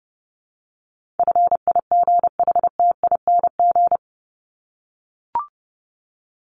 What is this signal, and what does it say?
Morse code, heard as a keyed tone, about 28 wpm, 700 Hz. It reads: FIGHTING
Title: ING Suffix - Top 100 Words with Base Words 30wpm